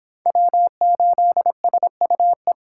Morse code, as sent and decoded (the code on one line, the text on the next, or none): .-- ---.. ... ..- .
W8SUE